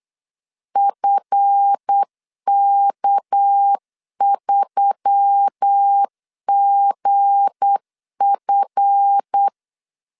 Morse code, heard as a keyed tone, about 8 wpm, 800 Hz. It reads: FK3GF
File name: LEZIONI / 7 GF9Z / 7 GF 8wpm 3108